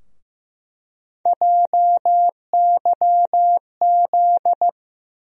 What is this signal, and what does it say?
Morse code, heard as a keyed tone, about 15 wpm, 700 Hz. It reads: JYZ
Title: USA Call Sign Postfixes - 3 Letters 15wpm